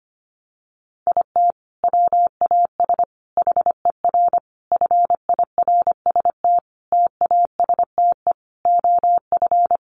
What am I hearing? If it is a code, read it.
Morse code, heard as a keyed tone, about 25 wpm, 700 Hz. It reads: IT WAS HER FIRST TASTE OF
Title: Sentences from Top 1100 Words 25wpm